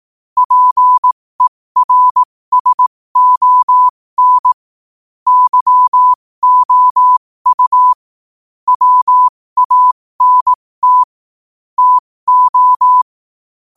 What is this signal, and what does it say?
Morse code, heard as a keyed tone, about 18 wpm, spaced as for 15 wpm, 1000 Hz. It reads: PERSON YOU WANT TO
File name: 15wpm